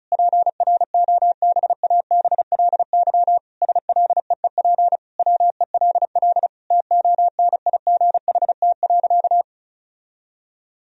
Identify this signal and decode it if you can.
Morse code, heard as a keyed tone, about 36 wpm, 700 Hz. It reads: PROBABLY SLEEP WELL TONIGHT.
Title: Sentences from Top 1000 Words - Review Part 4 - 35wpm